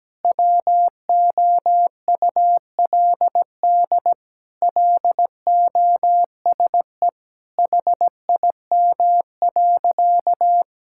Morse code, heard as a keyed tone, about 17 wpm, 700 Hz. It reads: WOULD LOSE HIM.
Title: Sentences from Top 1600 Words 17wpm